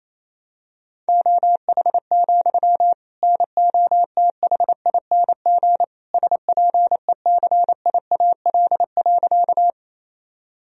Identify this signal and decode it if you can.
Morse code, heard as a keyed tone, about 28 wpm, 700 Hz. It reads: OH, NOTHING SPECIAL.
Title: Sentences from Top 1000 Words - Review Part 3 - 28wpm